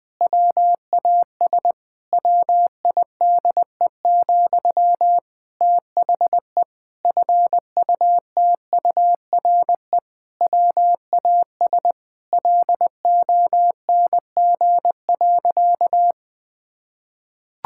Morse code, heard as a keyed tone, about 20 wpm, 700 Hz. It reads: WAS WIDE, THE FUTURE WAS LONG.
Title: Sentences from Top 1000 Words - Review Part 6 - 20wpm